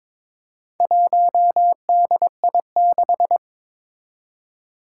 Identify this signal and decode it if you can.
Morse code, heard as a keyed tone, about 22 wpm, 700 Hz. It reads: 1DI6
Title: Four Challenging Letter-Number Combinations 22wpm